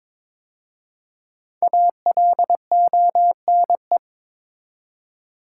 Morse code, heard as a keyed tone, about 22 wpm, 700 Hz. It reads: ALONE
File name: 5 Letter Words - First 3 Letters Given - 22wpm